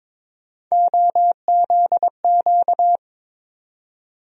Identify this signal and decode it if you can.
Morse code, heard as a keyed tone, about 22 wpm, 700 Hz. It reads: OZQ
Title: USA Call Sign Postfixes - 3 Letters 22wpm